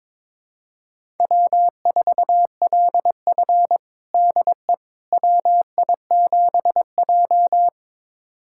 Morse code, heard as a keyed tone, about 22 wpm, 700 Hz. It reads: W4LF DE WI7J
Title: USA Callsign DE Callsign - 4 Characters - 22wpm